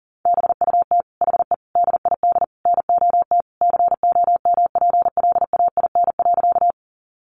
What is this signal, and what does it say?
Morse code, heard as a keyed tone, about 40 wpm, 700 Hz. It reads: BUT HE DID NOT COMPLAIN.
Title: Sentences from Top 1300 Words 40wpm